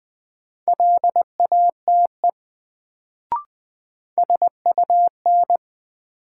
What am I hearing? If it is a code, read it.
Morse code, heard as a keyed tone, about 20 wpm, 700 Hz. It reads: LATE SUN
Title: Sentences from Top 300 Words - New Word Drill - 20wpm